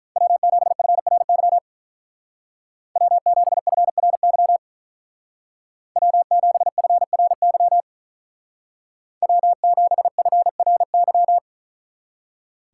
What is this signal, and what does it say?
Morse code, heard as a keyed tone, about 43 wpm, 700 Hz. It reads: W7FRY W7FRY W7FRY W7FRY
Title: Speed Racing - USA 1x3 Call Signs 35wpm